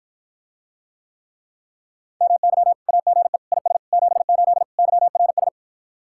Morse code, heard as a keyed tone, about 59 wpm, 700 Hz. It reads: MY AGE IS 78 YRS